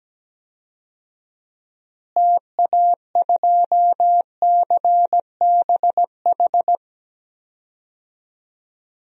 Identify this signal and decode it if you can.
Morse code, heard as a keyed tone, about 17 wpm, 700 Hz. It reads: TA2CBH
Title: DXCC-100 Call Signs - 6 Characters - Repeated 2x - 17wpm